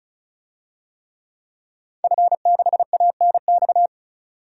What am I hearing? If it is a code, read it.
Morse code, heard as a keyed tone, about 36 wpm, 700 Hz. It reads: F6ANX
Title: DXCC-100 Call Signs - 5 Characters - 35wpm